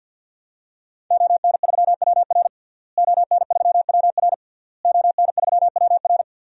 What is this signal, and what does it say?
Morse code, heard as a keyed tone, about 52 wpm, 700 Hz. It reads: ON3WR ON3WR ON3WR
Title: DXCC-100 Call Signs - 5 Characters - Repeated 3x - 50wpm